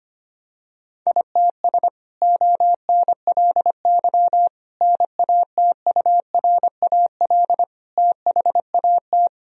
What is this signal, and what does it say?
Morse code, heard as a keyed tone, about 25 wpm, 700 Hz. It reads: ITS ONLY NATURAL THAT
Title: Sentences from Top 1000 Words - Review Part 4 - 25wpm